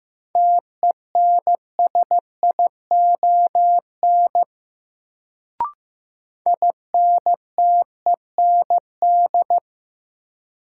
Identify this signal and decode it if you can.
Morse code, heard as a keyed tone, about 15 wpm, 700 Hz. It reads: TENSION INTEND
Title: Sentences from Top 1600 Words - New Word Drill Alt - 15wpm